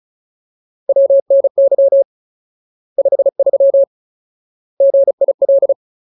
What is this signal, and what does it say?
Morse code, heard as a keyed tone, about 34 wpm, 550 Hz. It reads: WNY 53 GIL
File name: Ultimate Word List / 35WPM Ultimate